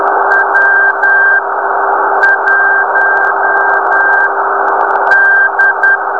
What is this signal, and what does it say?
Morse code, heard as a keyed tone, about 18 wpm, spaced as for 8 wpm, 1600 Hz, with noise. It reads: W1D
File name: beacon / 2014 02-09 nx0p